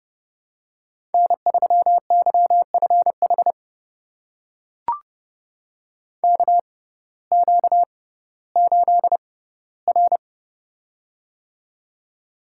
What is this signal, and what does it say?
Morse code, heard as a keyed tone, about 32 wpm, spaced as for 17 wpm, 700 Hz. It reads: N3YFH KQ8R
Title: USA Call Signs - 6x Character Spacing - 30wpm